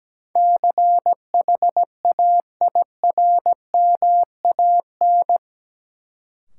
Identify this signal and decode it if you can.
Morse code, heard as a keyed tone, about 17 wpm, 700 Hz. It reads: CHAIRMAN
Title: Sets of 3 Words - New Word Drill - 17wpm